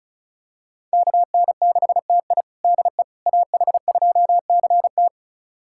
Morse code, heard as a keyed tone, about 34 wpm, 690 Hz. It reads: KN6TI DE AH2CT
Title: USA Callsign DE Callsign - 5 Characters - 35wpm